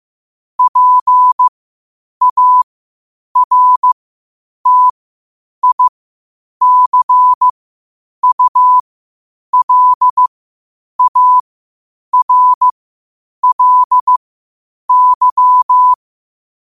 Morse code, heard as a keyed tone, about 15 wpm, spaced as for 8 wpm, 1000 Hz. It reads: PARTICULARLY
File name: qotdfiles / 8wpm